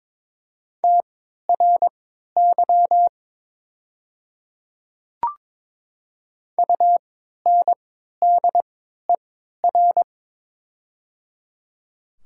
Morse code, heard as a keyed tone, about 22 wpm, spaced as for 13 wpm, 700 Hz. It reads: TRY UNDER
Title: 3x Character Spacing - Top 200 Words - 22wpm